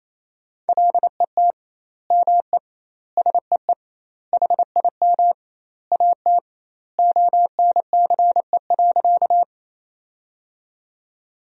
Morse code, heard as a keyed tone, about 28 wpm, 700 Hz. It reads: LET ME SEE HIM AT ONCE.
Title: 2x Word-Spacing - Sentences from Top 300 Words - 28wpm